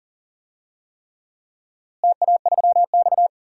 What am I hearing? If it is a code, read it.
Morse code, heard as a keyed tone, about 40 wpm, 700 Hz. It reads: TA3X
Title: DXCC-100 Call Signs - 4 Characters - Repeated 3x - 40wpm